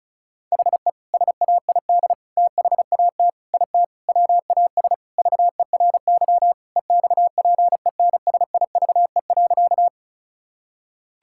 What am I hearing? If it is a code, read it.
Morse code, heard as a keyed tone, about 36 wpm, 700 Hz. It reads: HE SAID THAT IT WAS VERY EXPENSIVE.
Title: Sentences from Top 1000 Words - Review Part 6 - 35wpm